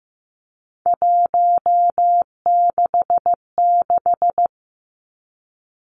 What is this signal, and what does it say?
Morse code, heard as a keyed tone, about 15 wpm, 700 Hz. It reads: 166
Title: Three Challenging Letter-Number Combinations 15wpm